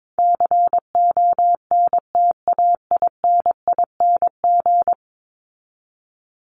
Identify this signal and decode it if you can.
Morse code, heard as a keyed tone, about 22 wpm, 700 Hz. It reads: CONTAINING
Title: ING Suffix - Top 100 Words with Base Words 22wpm